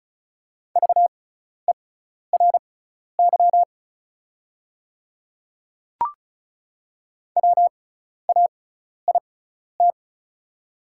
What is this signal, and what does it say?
Morse code, heard as a keyed tone, about 34 wpm, spaced as for 12 wpm, 700 Hz. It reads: VERY WAIT
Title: 6x Character Spacing - Top 500 Words - 35wpm